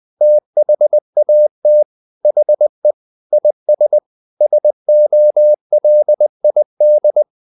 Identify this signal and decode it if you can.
Morse code, heard as a keyed tone, about 20 wpm, 600 Hz. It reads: THAT HE IS SOLID
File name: the old man and the sea part 2 20wpm